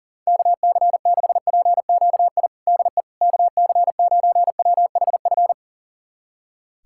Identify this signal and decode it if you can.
Morse code, heard as a keyed tone, about 37 wpm, 700 Hz. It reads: KC6PQI DE KC9WHF
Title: USA Callsign DE Callsign - 6 Characters - 40wpm